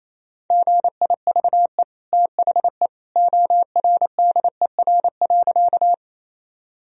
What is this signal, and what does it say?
Morse code, heard as a keyed tone, about 27 wpm, 700 Hz. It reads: GIVE THE ORDER.
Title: Sentences from Top 1000 Words - Review Part 2 - 28wpm